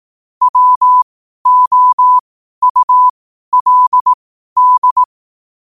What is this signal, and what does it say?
Morse code, heard as a keyed tone, about 18 wpm, spaced as for 12 wpm, 1000 Hz. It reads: WOULD
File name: qotdfiles / 12wpm